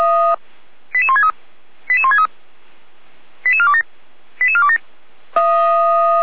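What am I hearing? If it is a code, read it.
Five-tone selective call (ZVEI), heard as quick runs of five tones, tones of about 70 ms. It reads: 88163 88163 88427 88427
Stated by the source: ZVEI five-tone selective call received on SDR.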